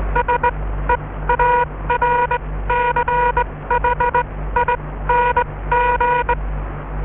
Morse code, heard as a keyed tone, about 20 wpm, spaced as for 15 wpm, 960 Hz. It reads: SEARCHING